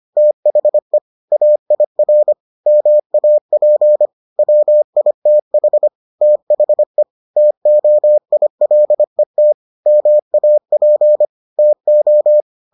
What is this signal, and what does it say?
Morse code, heard as a keyed tone, about 25 wpm, 600 Hz. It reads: THE AIR MAP WITH THE TOILET MAP TO